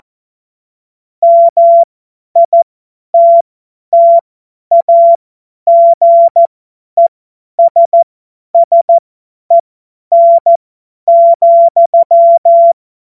Morse code, heard as a keyed tone, about 14 wpm, spaced as for 10 wpm, 680 Hz. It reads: MITTAGESSEN,
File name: Morsekurs / Witze / Witz 23. August 2022 50 F70